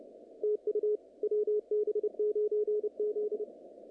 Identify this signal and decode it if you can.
Morse code, heard as a keyed tone, about 29 wpm, 440 Hz. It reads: TU WB9Z
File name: contests / naqp-cw-k5tr-op-k5pi- -2200